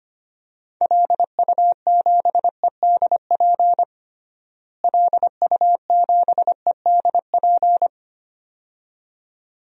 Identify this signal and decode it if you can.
Morse code, heard as a keyed tone, about 25 wpm, 700 Hz. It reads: LU7EDP LU7EDP
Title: DXCC-100 Call Signs - 6 Characters - Repeated 2x - 25wpm